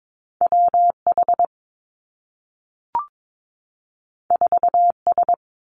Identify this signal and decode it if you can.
Morse code, heard as a keyed tone, about 22 wpm, 700 Hz. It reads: W H 4 S
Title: Two Challenging Letter-Number Combinations 22wpm